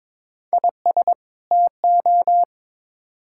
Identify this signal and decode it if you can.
Morse code, heard as a keyed tone, about 22 wpm, 700 Hz. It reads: IS TO
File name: Build A Phrase - 2-Word Phrases - 22wpm